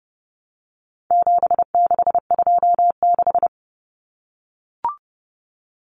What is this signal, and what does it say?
Morse code, heard as a keyed tone, about 30 wpm, 700 Hz. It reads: 7626
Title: Numbers - 4-Digit 30wpm